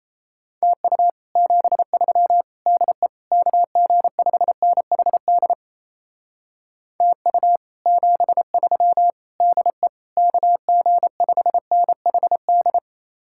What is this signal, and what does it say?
Morse code, heard as a keyed tone, about 35 wpm, 700 Hz. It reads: TU 73 DE KG5NHD TU 73 DE KG5NHD
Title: Speed-Racing Common QSO Phrases - 28wpm